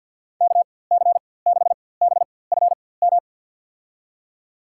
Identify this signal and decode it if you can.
Morse code, heard as a keyed tone, about 48 wpm, spaced as for 23 wpm, 700 Hz. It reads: KC6BFM
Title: USA Call Signs - 6 Characters - 4x Character Spacing - 50wpm